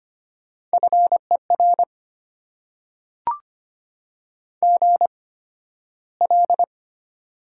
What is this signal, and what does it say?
Morse code, heard as a keyed tone, about 25 wpm, spaced as for 13 wpm, 700 Hz. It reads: FER GL
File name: Top 90 QSO Elements - 8x Character Spacing - 25wpm